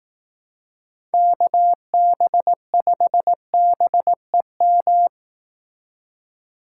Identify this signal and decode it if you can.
Morse code, heard as a keyed tone, about 18 wpm, 700 Hz. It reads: KB5BEM
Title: Speed Racing - USA 2x3 Call Signs 15wpm